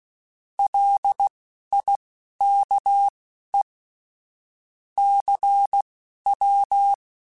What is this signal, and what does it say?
Morse code, heard as a keyed tone, about 16 wpm, spaced as for 11 wpm, 800 Hz. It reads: LIKE CW
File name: QSOs / QSOs-mp3s / WD0BGP 2@10